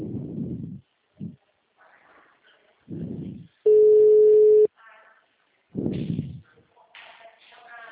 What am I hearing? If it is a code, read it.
Ringback tone (Europe, 425 Hz).
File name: out- -2001- - - .38927